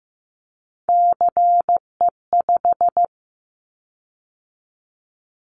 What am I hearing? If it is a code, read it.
Morse code, heard as a keyed tone, about 15 wpm, 700 Hz. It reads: CE5
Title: Callsign Prefixes for Top 100 DXCC - 15wpm